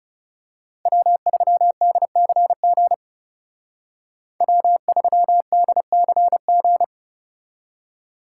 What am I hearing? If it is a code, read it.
Morse code, heard as a keyed tone, about 34 wpm, 700 Hz. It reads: W3DCG W3DCG
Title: Speed Racing - USA 1x3 Call Signs 30wpm